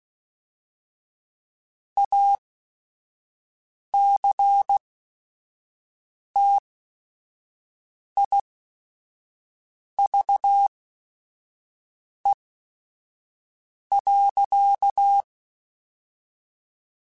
Morse code, heard as a keyed tone, about 16 wpm, spaced as for 5 wpm, 800 Hz. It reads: ACTIVE.
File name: Technical Subjects / Technical Subjects-mp3s / OR Gate @5